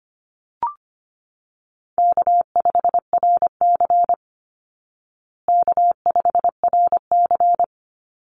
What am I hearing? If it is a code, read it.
Morse code, heard as a keyed tone, about 25 wpm, 700 Hz. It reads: K5RC K5RC
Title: Easy Call Signs Repeated 2x - 25wpm